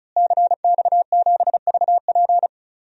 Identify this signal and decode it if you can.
Morse code, heard as a keyed tone, about 36 wpm, 700 Hz. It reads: CX7VP
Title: DXCC-100 Call Signs - 5 Characters - Repeated 2x - 35wpm